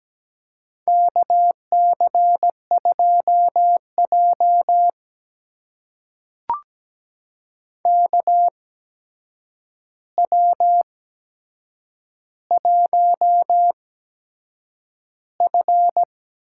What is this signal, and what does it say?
Morse code, heard as a keyed tone, about 17 wpm, spaced as for 7 wpm, 700 Hz. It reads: KC2J KW1F
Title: USA Call Signs - 4 Characters - 8x Character Spacing - 17wpm